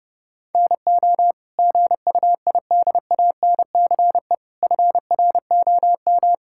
Morse code, heard as a keyed tone, about 30 wpm, 700 Hz. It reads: NO GUIDANCE FROM
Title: Sentences from Top 1600 Words 30wpm